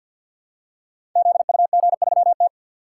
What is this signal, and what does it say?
Morse code, heard as a keyed tone, about 49 wpm, 700 Hz. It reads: ZUG3T
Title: Code Groups - 5 Letter:Numbers - 50wpm